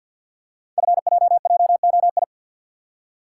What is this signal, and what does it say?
Morse code, heard as a keyed tone, about 47 wpm, 700 Hz. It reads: FJJOI
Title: Code Groups - 5 Letters - 50wpm